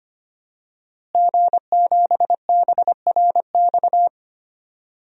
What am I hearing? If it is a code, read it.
Morse code, heard as a keyed tone, about 25 wpm, 700 Hz. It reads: G7BRX
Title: DXCC-100 Call Signs - 5 Characters - Repeated 2x - 25wpm